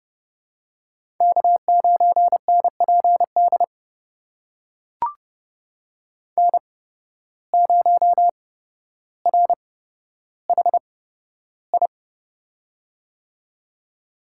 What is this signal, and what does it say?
Morse code, heard as a keyed tone, about 30 wpm, spaced as for 13 wpm, 700 Hz. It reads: K9NPD N0RHI
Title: USA Call Signs - 5 Characters - 8x Character Spacing - 30wpm